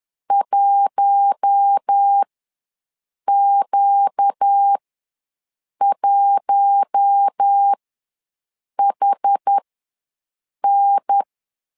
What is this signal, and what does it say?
Morse code, heard as a keyed tone, about 10 wpm, spaced as for 6 wpm, 800 Hz. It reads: 1Q1HN
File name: LEZIONI / 4 Q1 ws 10wpm 2538